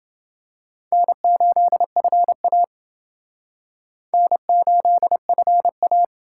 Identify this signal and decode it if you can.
Morse code, heard as a keyed tone, about 28 wpm, 700 Hz. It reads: N8FA N8FA